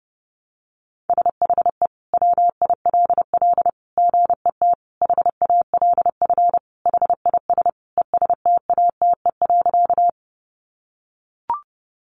Morse code, heard as a keyed tone, about 30 wpm, 700 Hz. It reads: SHE WILL GET HALF HIS ESTATE.
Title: Sentences from Top 1200 Words 30wpm